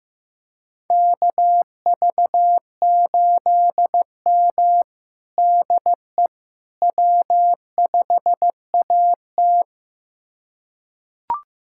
Morse code, heard as a keyed tone, about 15 wpm, 700 Hz. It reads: KV8M DE W5AT
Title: USA Callsign DE Callsign - 4 Characters - 15wpm